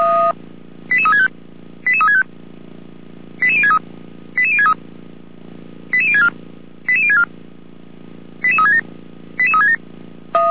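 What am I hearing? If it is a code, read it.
Five-tone selective call (ZVEI), heard as quick runs of five tones, tones of about 70 ms. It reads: 88365 88365 88063 88063 88064 88064 88367 88367
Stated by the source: ZVEI five-tone selective call received on an SDR, recorded by the Feuerwehr Orlamünde fire brigade.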